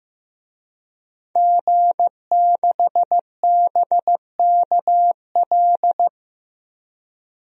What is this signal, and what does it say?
Morse code, heard as a keyed tone, about 15 wpm, 700 Hz. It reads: G6BKL
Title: DXCC-100 Call Signs - 5 Characters - Repeated 3x - 15wpm